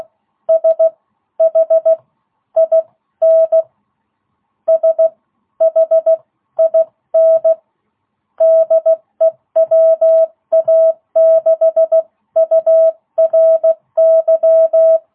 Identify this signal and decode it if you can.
Morse code, heard as a keyed tone, about 16 wpm, spaced as for 13 wpm, 660 Hz. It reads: SHIN SHIN DE WA6URY